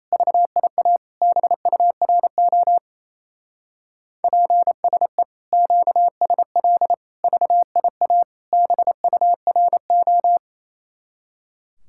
Morse code, heard as a keyed tone, about 35 wpm, 700 Hz. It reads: VIA BURO PSE QSL VIA BURO
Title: Speed-Racing Common QSO Phrases - 28wpm